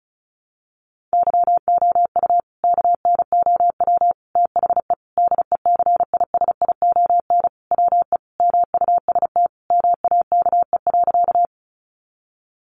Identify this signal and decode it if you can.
Morse code, heard as a keyed tone, about 34 wpm, 700 Hz. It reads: YOU KNOW THE DECISION WE MUST MAKE.
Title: Sentences from Top 1000 Words 35wpm